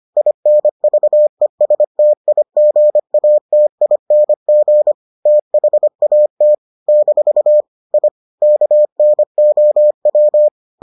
Morse code, heard as a keyed tone, about 25 wpm, 600 Hz. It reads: INVESTIGATING THAT - I KNOW